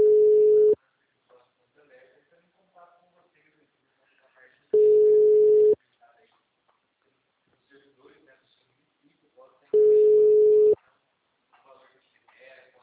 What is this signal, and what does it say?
Ringback tone (Europe, 425 Hz), with speech.
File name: out- -2001- - - .3532